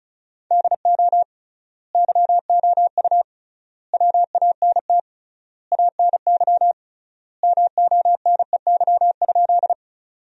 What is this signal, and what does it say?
Morse code, heard as a keyed tone, about 36 wpm, 700 Hz. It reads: DO YOU WANT ANY MONEY?
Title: 3x Word-Spacing - Sentences from Top 400 Words - 35wpm